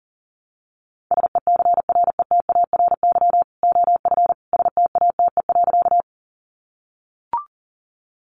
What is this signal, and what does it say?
Morse code, heard as a keyed tone, about 40 wpm, 700 Hz. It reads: SECRETARY OF STATE.
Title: Sentences from Top 1300 Words 40wpm